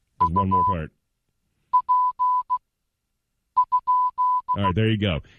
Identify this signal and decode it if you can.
Morse code, heard as a keyed tone, about 15 wpm, spaced as for 7 wpm, 1000 Hz, with speech. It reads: UP?